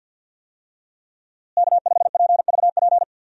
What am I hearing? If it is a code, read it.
Morse code, heard as a keyed tone, about 51 wpm, 700 Hz. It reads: K5PVP
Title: USA Call Signs - 5 Characters - Repeated 3x - 50wpm